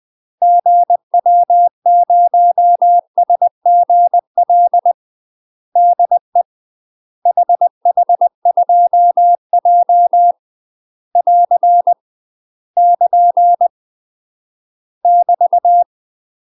Morse code, heard as a keyed tone, about 20 wpm, 700 Hz. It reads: GW0SGL DE HH2J <AR> <KN> <BT>
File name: UK MASTER QSOs TRAINING / 20WPM / UK MASTER QSOs 0035